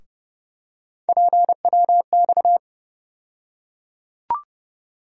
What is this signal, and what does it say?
Morse code, heard as a keyed tone, about 32 wpm, 700 Hz. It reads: PWX
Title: USA Call Sign Postfixes - 3 Letters 30wpm